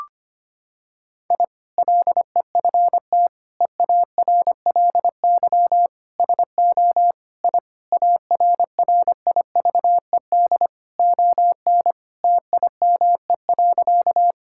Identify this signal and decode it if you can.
Morse code, heard as a keyed tone, about 25 wpm, 700 Hz. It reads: I LEFT EARLY SO I ARRIVED ON TIME.
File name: practice / 25wpm / Sentences with 2-Word Phrases - 25wpm